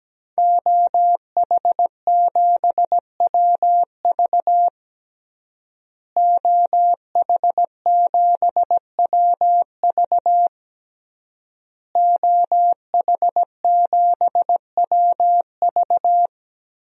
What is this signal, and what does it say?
Morse code, heard as a keyed tone, about 17 wpm, 700 Hz. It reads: OH7WV OH7WV OH7WV
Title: DXCC-100 Call Signs - 5 Characters - Repeated 3x - 17wpm